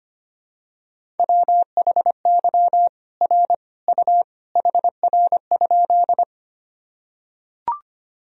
Morse code, heard as a keyed tone, about 25 wpm, 700 Hz. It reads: WHY R U HR?
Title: Sentences from Top 200 Words and Abbreviations 25wpm